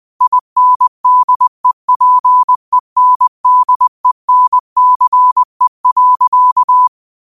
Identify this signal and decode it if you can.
Morse code, heard as a keyed tone, about 20 wpm, 1000 Hz. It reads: INDEPENDENCE.
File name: qotdfiles / 20wpm